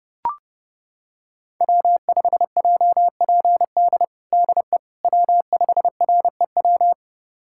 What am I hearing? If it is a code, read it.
Morse code, heard as a keyed tone, about 30 wpm, 700 Hz. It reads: W5JPD DE W5REW
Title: USA Callsign DE Callsign - 5 Characters - 30wpm